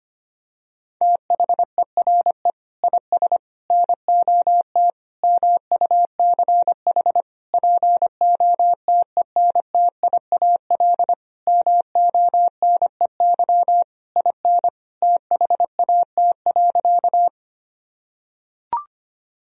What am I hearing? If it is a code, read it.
Morse code, heard as a keyed tone, about 25 wpm, 700 Hz. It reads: THERE IS NOT MUCH POTENTIAL MONEY IN THAT.
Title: Sentences from Top 1000 Words - Review Part 4 - 25wpm